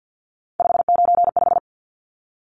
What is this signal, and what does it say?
Morse code, heard as a keyed tone, about 50 wpm, 700 Hz. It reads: 595
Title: Numbers - 3-Digit 50wpm